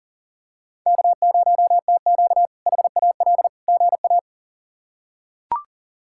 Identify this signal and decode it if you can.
Morse code, heard as a keyed tone, about 40 wpm, 700 Hz. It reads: K0TQ HAL GA
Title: K1USN SST QSO Practice - 40wpm